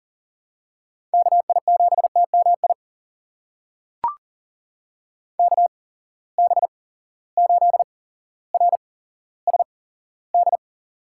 Morse code, heard as a keyed tone, about 37 wpm, spaced as for 17 wpm, 700 Hz. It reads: KI7TMI KB8RSD
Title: USA Call Signs - 6 Characters - 8x Character Spacing - 40wpm